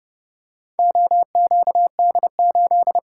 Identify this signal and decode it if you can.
Morse code, heard as a keyed tone, about 29 wpm, 700 Hz. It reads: OQD8